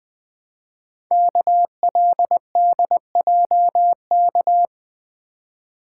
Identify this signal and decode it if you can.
Morse code, heard as a keyed tone, about 20 wpm, 700 Hz. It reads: KLDJK
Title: Code Groups - 5 Letters - 20wpm